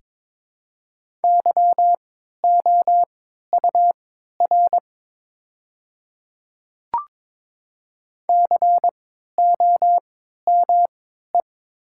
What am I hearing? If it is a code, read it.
Morse code, heard as a keyed tone, about 22 wpm, spaced as for 13 wpm, 700 Hz. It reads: YOUR COME
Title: 3x Character Spacing - Top 100 Words - 22wpm